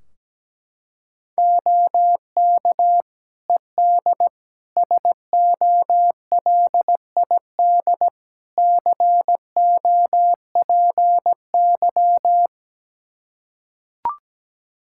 Morse code, heard as a keyed tone, about 17 wpm, 700 Hz. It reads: OK ED SOLID COPY
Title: K1USN SST QSO Practice 17wpm